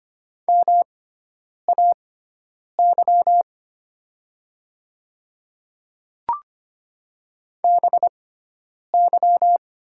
Morse code, heard as a keyed tone, about 25 wpm, spaced as for 9 wpm, 700 Hz. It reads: MAY BY
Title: 6x Character Spacing - Top 100 Words - 25wpm